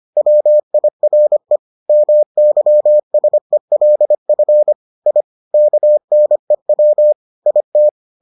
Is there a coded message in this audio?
Yes: Morse code from 0.2 to 7.9 s, heard as a keyed tone, about 25 wpm, 600 Hz. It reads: WIRE MYSELF I KNEW IT